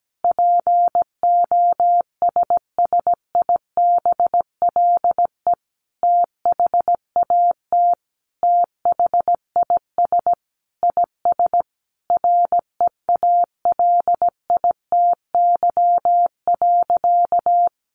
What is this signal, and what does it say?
Morse code, heard as a keyed tone, about 17 wpm, 700 Hz. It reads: POSSIBLE THAT THIS IS REALITY.
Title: Sentences from Top 1200 Words 17wpm